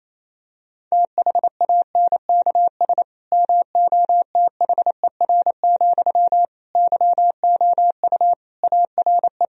Morse code, heard as a keyed tone, about 28 wpm, 700 Hz. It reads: THANKS MOTHER, YOU ARE
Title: Sentences from Top 1000 Words - Review Part 3 - 28wpm